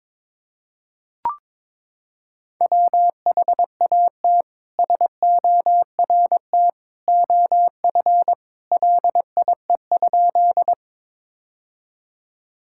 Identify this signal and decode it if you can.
Morse code, heard as a keyed tone, about 22 wpm, 700 Hz. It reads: WHAT SORT OF LIE?
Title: Sentences from Top 1600 Words 22wpm